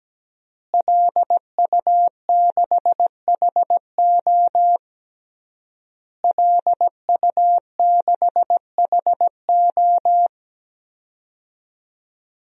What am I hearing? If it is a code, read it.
Morse code, heard as a keyed tone, about 17 wpm, 700 Hz. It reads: LU6HO LU6HO
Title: DXCC-100 Call Signs - 5 Characters - Repeated 3x - 17wpm